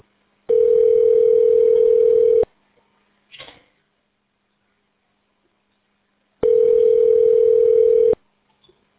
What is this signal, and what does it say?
Ringback tone (North America), 440 + 480 Hz.